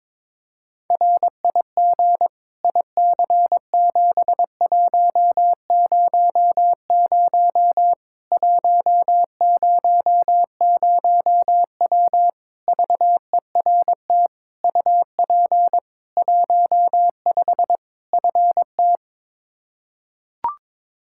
Morse code, heard as a keyed tone, about 22 wpm, 700 Hz. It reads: RIG IC7100 100W VERT UP 15 FT
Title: K1USN SST QSO Practice 22wpm